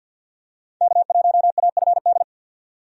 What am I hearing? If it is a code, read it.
Morse code, heard as a keyed tone, about 50 wpm, 700 Hz. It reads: K1AFD
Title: USA Call Signs - 5 Characters - 50wpm